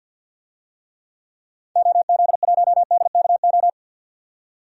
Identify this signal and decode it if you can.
Morse code, heard as a keyed tone, about 50 wpm, 700 Hz. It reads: OZ1DKO
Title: DXCC-100 Call Signs - 6 Characters - 50wpm